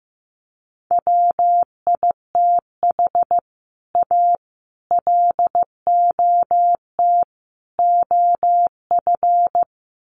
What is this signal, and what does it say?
Morse code, heard as a keyed tone, about 14 wpm, 700 Hz. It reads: WITH A LOT OF
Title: Sets of 4 Words 15wpm